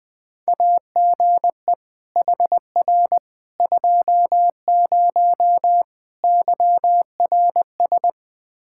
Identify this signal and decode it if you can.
Morse code, heard as a keyed tone, about 20 wpm, 700 Hz. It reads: AGE HR 20 YRS
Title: Practice QSO - Exchange 3 - More Info - 20wpm